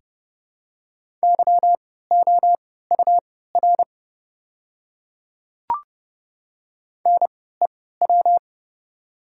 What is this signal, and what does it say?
Morse code, heard as a keyed tone, about 32 wpm, spaced as for 17 wpm, 700 Hz. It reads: YOUR NEW
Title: 3x Character Spacing - Top 300 Words - 30wpm